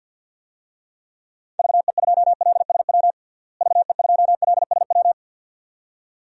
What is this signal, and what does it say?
Morse code, heard as a keyed tone, about 53 wpm, 700 Hz. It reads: VE2LSW VE2LSW
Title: DXCC-100 Call Signs - 6 Characters - Repeated 2x - 50wpm